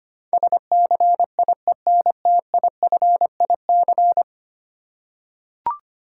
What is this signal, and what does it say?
Morse code, heard as a keyed tone, about 25 wpm, 700 Hz. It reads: SCIENTIFIC
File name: Ten Letter Words - 25wpm